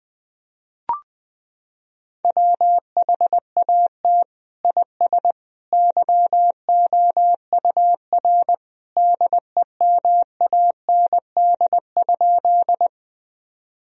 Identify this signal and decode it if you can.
Morse code, heard as a keyed tone, about 21 wpm, 700 Hz. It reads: WHAT IS YOUR DEMAND?
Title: Sentences from Top 1000 Words - Review Part 4 - 20wpm